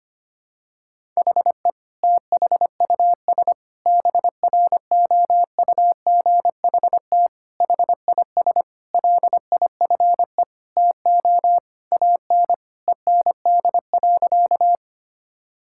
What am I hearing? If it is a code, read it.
Morse code, heard as a keyed tone, about 25 wpm, 700 Hz. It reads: HE THUS BROUGHT HIS LIFE TO AN END.
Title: Sentences from Top 1000 Words - Review Part 5 - 25wpm